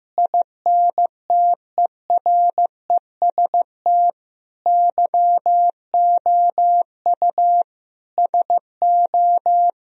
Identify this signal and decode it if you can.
Morse code, heard as a keyed tone, about 16 wpm, 700 Hz. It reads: INTEREST YOU SO
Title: Sentences from Top 1000 Words - Review Part 2 - 15wpm